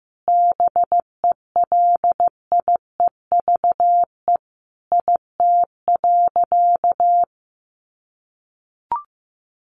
Morse code, heard as a keyed tone, about 15 wpm, 700 Hz. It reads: BELIEVE IT.
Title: Sentences from Top 1000 Words 15wpm